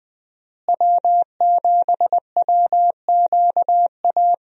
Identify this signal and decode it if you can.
Morse code, heard as a keyed tone, about 20 wpm, 700 Hz. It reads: W7WQA
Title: USA Call Signs - 5 Characters - Repeated 3x - 20wpm